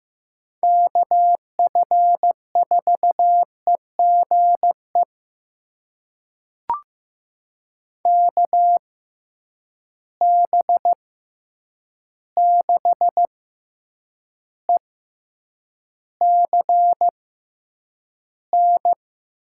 Morse code, heard as a keyed tone, about 15 wpm, spaced as for 8 wpm, 700 Hz. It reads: KF4EGE KB6ECN
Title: USA Call Signs - 6 Characters - 6x Character Spacing - 15wpm